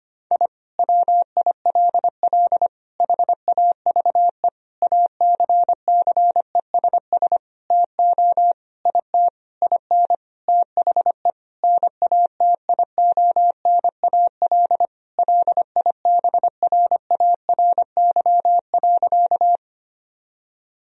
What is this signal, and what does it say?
Morse code, heard as a keyed tone, about 25 wpm, 700 Hz. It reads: I WILL HAVE ACCESS TO IT IN THE NATIONAL LIBRARY.
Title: Sentences from Top 1000 Words - Review Part 4 - 25wpm